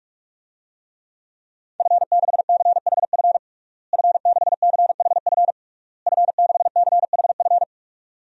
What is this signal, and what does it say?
Morse code, heard as a keyed tone, about 41 wpm, 700 Hz. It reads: F6CHF F6CHF F6CHF
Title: DXCC-100 Call Signs - 5 Characters - Repeated 3x - 45wpm